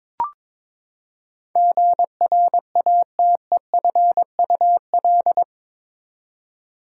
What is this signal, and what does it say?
Morse code, heard as a keyed tone, about 22 wpm, 700 Hz. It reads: GRATEFUL
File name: practice / 22wpm / Eight Letter Words - 22wpm